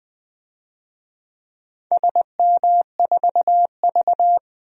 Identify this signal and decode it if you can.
Morse code, heard as a keyed tone, about 20 wpm, 700 Hz. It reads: SM4V
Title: DXCC-100 Call Signs - 4 Characters - 20wpm